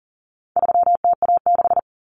Morse code, heard as a keyed tone, about 40 wpm, 700 Hz. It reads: 3TA6
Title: Common Cut Numbers - 4-Digit 40wpm